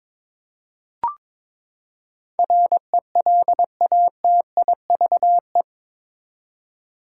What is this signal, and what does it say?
Morse code, heard as a keyed tone, about 22 wpm, 700 Hz. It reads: RELATIVE
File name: practice / 22wpm / Eight Letter Words - 22wpm